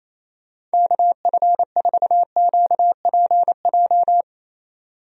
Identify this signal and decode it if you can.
Morse code, heard as a keyed tone, about 28 wpm, 700 Hz. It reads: KF4QPJ
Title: USA Call Signs - 6 Characters - 28wpm